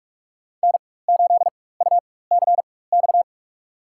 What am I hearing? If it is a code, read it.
Morse code, heard as a keyed tone, about 44 wpm, spaced as for 21 wpm, 700 Hz. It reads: N8UCX
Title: USA Call Signs - 5 Characters - 4x Character Spacing - 45wpm